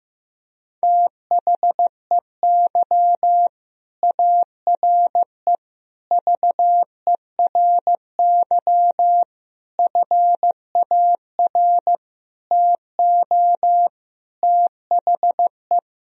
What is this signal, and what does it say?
Morse code, heard as a keyed tone, about 15 wpm, 700 Hz. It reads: THEY ARE VERY FAR TO THE
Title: Sentences from Top 1000 Words - Review Part 2 - 15wpm